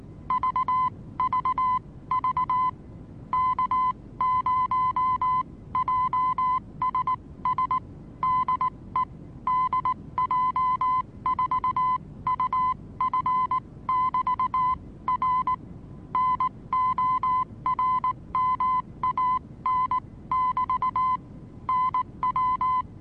Morse code, heard as a keyed tone, about 19 wpm, spaced as for 16 wpm, 1020 Hz. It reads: VVV K0JSS DE DJ4UF<BT>R NORMAN<BT> NW
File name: repos / Morsekurs / DJ4UF Morsen26